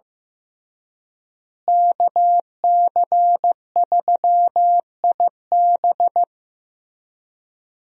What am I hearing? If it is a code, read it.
Morse code, heard as a keyed tone, about 15 wpm, 700 Hz. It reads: KC3IB
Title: USA Call Signs - 5 Characters - Repeated 3x - 15wpm